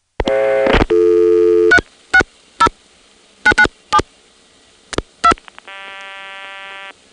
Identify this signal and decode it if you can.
DTMF, heard as telephone touch tones, tones of about 70 ms. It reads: 360997 2